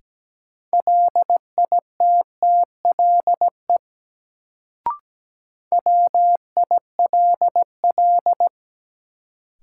Morse code, heard as a keyed tone, about 17 wpm, 700 Hz. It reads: LITTLE WILL
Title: Sets of 2 Words - New Word Drill - 17wpm